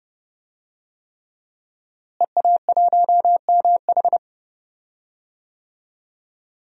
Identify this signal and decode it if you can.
Morse code, heard as a keyed tone, about 31 wpm, 700 Hz. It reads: EA1MH
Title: DXCC-100 Call Signs - 5 Characters - Repeated 3x - 30wpm